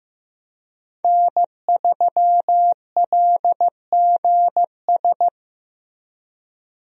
Morse code, heard as a keyed tone, about 16 wpm, 700 Hz. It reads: N3LGS
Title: USA Call Signs - 5 Characters - Repeated 3x - 15wpm